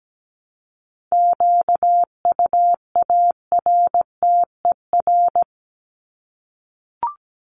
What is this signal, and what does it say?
Morse code, heard as a keyed tone, about 17 wpm, 700 Hz. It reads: QUARTER
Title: Top 1001-1250 Words 17wpm